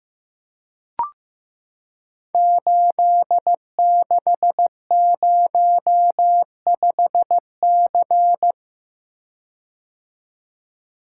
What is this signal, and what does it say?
Morse code, heard as a keyed tone, about 15 wpm, 700 Hz. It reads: 8605C
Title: Code Groups - 5 Letter:Numbers - 15wpm